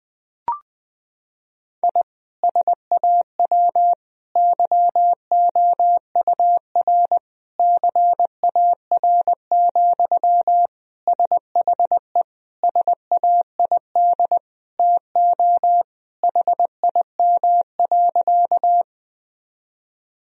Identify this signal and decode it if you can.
Morse code, heard as a keyed tone, about 20 wpm, 700 Hz. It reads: I SAW YOUR CAR, SHE SAID TO HIM.
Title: Sentences from Top 1000 Words - Review Part 1 - 20wpm